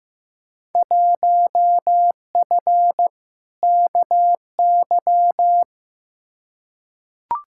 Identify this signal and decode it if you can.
Morse code, heard as a keyed tone, about 15 wpm, 700 Hz. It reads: 1F KY
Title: Field Day Exchanges Encore - 15wpm